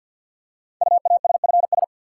Morse code, heard as a keyed tone, about 46 wpm, 700 Hz. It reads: UASFS
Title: Code Groups - 5 Letters - 50wpm